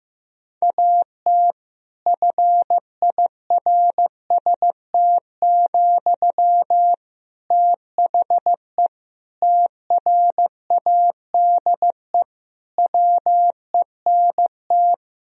Morse code, heard as a keyed tone, about 15 wpm, 700 Hz. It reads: AT FIRST, THE TRADE WENT
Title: Sentences from Top 1000 Words - Review Part 4 - 15wpm